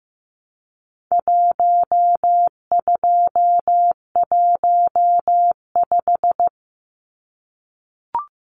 Morse code, heard as a keyed tone, about 15 wpm, 700 Hz. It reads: 1215
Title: Numbers - 4-Digit 15wpm